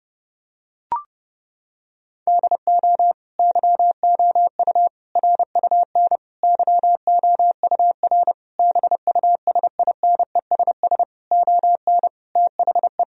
Morse code, heard as a keyed tone, about 30 wpm, 700 Hz. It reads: DO YOU RUN YOUR BUSINESS ON THE
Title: Sentences from Top 1000 Words - Review Part 4 - 30wpm